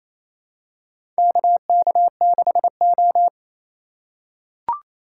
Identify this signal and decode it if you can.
Morse code, heard as a keyed tone, about 28 wpm, 700 Hz. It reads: KK6O
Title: USA Call Signs - 4 Characters - 8x Character Spacing - 28wpm